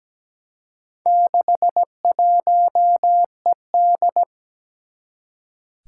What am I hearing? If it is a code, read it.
Morse code, heard as a keyed tone, about 17 wpm, 700 Hz. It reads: 61ED
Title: Four Challenging Letter-Number Combinations 17wpm